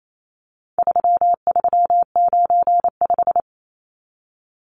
Morse code, heard as a keyed tone, about 28 wpm, 700 Hz. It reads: 3395
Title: Numbers - 4-Digit 28wpm